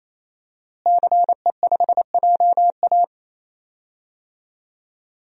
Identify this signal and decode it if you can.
Morse code, heard as a keyed tone, about 28 wpm, 700 Hz. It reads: CE5JA
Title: DXCC-100 Call Signs - 5 Characters - Repeated 3x - 28wpm